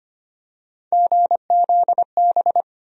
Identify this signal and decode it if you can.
Morse code, heard as a keyed tone, about 25 wpm, 700 Hz. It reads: GZB
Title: USA Call Sign Postfixes - 3 Letters 25wpm